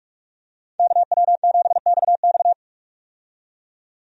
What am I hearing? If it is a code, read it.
Morse code, heard as a keyed tone, about 47 wpm, 700 Hz. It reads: KW7XX
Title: USA Call Signs - 5 Characters - 45wpm